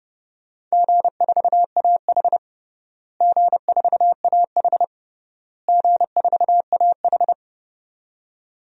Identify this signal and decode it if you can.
Morse code, heard as a keyed tone, about 31 wpm, 700 Hz. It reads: G4AH G4AH G4AH
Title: DXCC-100 Call Signs - 4 Characters - Repeated 3x - 30wpm